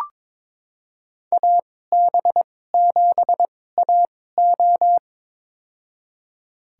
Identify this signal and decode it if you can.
Morse code, heard as a keyed tone, about 22 wpm, spaced as for 16 wpm, 700 Hz. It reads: AB7AO
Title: USA Call Signs - 5 Characters - 2x Character Spacing - 22wpm